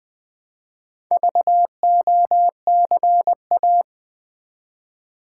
Morse code, heard as a keyed tone, about 20 wpm, 700 Hz. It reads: VOCA
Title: Build A Word - 1 Letter At A Time - 3 to 5-Letter Words - 20wpm